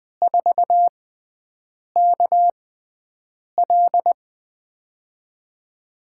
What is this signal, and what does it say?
Morse code, heard as a keyed tone, about 20 wpm, spaced as for 7 wpm, 700 Hz. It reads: 4KL
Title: USA Call Signs - 5 Characters - 6x Character Spacing - 20wpm